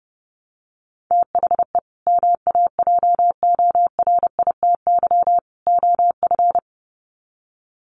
Morse code, heard as a keyed tone, about 30 wpm, 700 Hz. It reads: THE MAJORITY OF
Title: Sets of 3 Words 30wpm